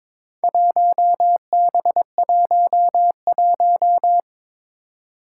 Morse code, heard as a keyed tone, about 22 wpm, 700 Hz. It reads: 1B11
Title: Four Challenging Letter-Number Combinations 22wpm